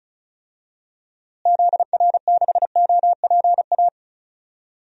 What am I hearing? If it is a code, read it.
Morse code, heard as a keyed tone, about 36 wpm, 700 Hz. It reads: ZR6OPA
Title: DXCC-100 Call Signs - 6 Characters - Repeated 2x - 35wpm